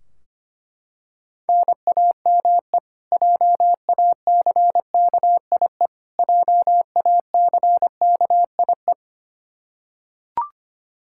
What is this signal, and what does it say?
Morse code, heard as a keyed tone, about 25 wpm, 700 Hz. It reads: NAME JACKIE JACKIE
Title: Practice QSO - Exchange 1 - Beginning - 25wpm